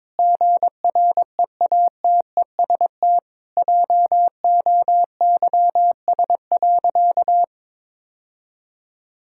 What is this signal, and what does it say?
Morse code, heard as a keyed tone, about 22 wpm, 700 Hz. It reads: GREATEST JOYS.
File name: Difficult Sentences with Hints - 22wpm